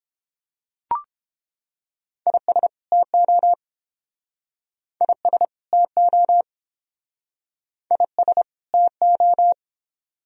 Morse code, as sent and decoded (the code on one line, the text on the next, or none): .. ... / - --- / .. ... / - --- / .. ... / - ---
IS TO IS TO IS TO